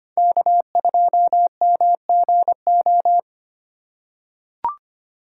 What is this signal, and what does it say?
Morse code, heard as a keyed tone, about 25 wpm, 700 Hz. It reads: K2MGO
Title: USA Call Signs - 3x Character Spacing - 25wpm